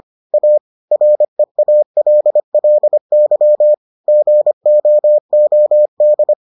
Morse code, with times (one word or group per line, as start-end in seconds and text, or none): 0.3-0.6 A
0.9-3.7 REALLY
4.1-6.3 GOOD